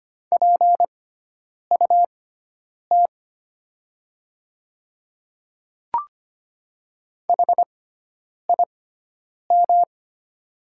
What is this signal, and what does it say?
Morse code, heard as a keyed tone, about 25 wpm, spaced as for 9 wpm, 700 Hz. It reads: PUT HIM
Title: 6x Character Spacing - Top 300 Words - 25wpm